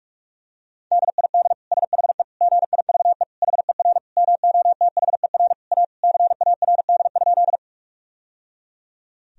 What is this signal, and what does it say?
Morse code, heard as a keyed tone, about 46 wpm, 700 Hz. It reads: DID SHE GIVE HER MOTHER A CARD?
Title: Sentences from Top 1000 Words - Review Part 4 - 45wpm